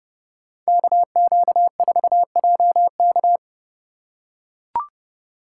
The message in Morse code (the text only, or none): KQ4JK